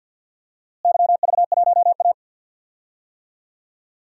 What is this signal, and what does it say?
Morse code, heard as a keyed tone, about 50 wpm, 700 Hz. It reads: YV1A
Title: DXCC-100 Call Signs - 4 Characters - Repeated 3x - 50wpm